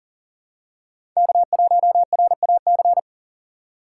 Morse code, heard as a keyed tone, about 40 wpm, 700 Hz. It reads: K1RAC
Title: USA Call Signs - 5 Characters - 3x Character Spacing - 40wpm